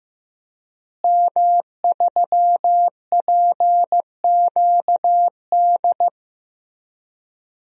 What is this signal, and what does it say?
Morse code, heard as a keyed tone, about 15 wpm, 700 Hz. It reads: M3PQD
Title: DXCC-100 Call Signs - 5 Characters - Repeated 3x - 15wpm